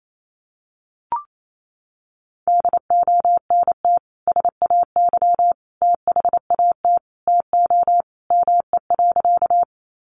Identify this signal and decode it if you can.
Morse code, heard as a keyed tone, about 28 wpm, 700 Hz. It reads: DONT SAY THAT TO ME.
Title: Sentences from Top 300 Words 28wpm